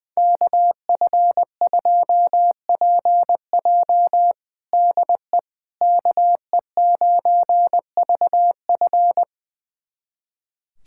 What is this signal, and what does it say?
Morse code, heard as a keyed tone, about 21 wpm, 700 Hz. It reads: KF2PJ DE KE9VF
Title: USA Callsign DE Callsign - 5 Characters - 20wpm